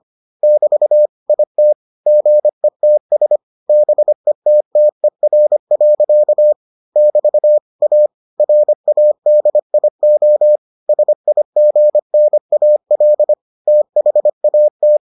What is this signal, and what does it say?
Morse code, heard as a keyed tone, about 25 wpm, 600 Hz. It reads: <BT> IT GETS BETTER. <BT> A RADIO SIGNAL THAT